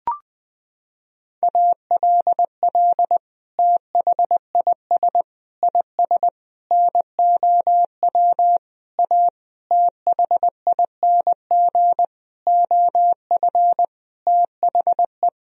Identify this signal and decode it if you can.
Morse code, heard as a keyed tone, about 21 wpm, 700 Hz. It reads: ALL THIS IS NOW A THING OF THE
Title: Sentences from Top 1000 Words - Review Part 3 - 20wpm